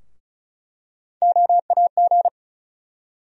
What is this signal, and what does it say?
Morse code, heard as a keyed tone, about 34 wpm, 700 Hz. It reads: OAG
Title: USA Call Sign Postfixes - 3 Letters 35wpm